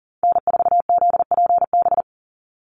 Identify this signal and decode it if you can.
Morse code, heard as a keyed tone, about 42 wpm, 700 Hz. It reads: N4ZPB
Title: USA Call Signs 40wpm